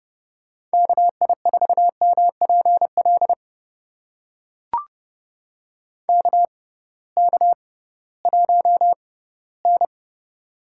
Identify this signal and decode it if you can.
Morse code, heard as a keyed tone, about 30 wpm, spaced as for 18 wpm, 700 Hz. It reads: KI4MPL KK1N
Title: USA Call Signs - 6x Character Spacing - 30wpm